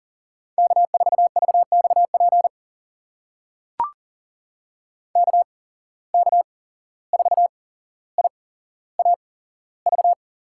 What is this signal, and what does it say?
Morse code, heard as a keyed tone, about 42 wpm, spaced as for 16 wpm, 700 Hz. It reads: K4VXP KK4IAV
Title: USA Call Signs - 8x Character Spacing - 40wpm